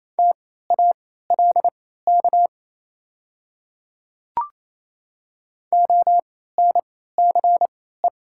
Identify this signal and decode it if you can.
Morse code, heard as a keyed tone, about 28 wpm, spaced as for 16 wpm, 700 Hz. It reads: TALK ONCE
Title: 3x Character Spacing - Top 500 Words - 28wpm